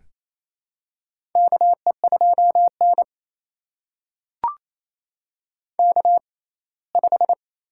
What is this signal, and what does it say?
Morse code, heard as a keyed tone, about 28 wpm, spaced as for 19 wpm, 700 Hz. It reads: KE2N K5
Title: USA Call Signs - 4 Characters - 6x Character Spacing - 28wpm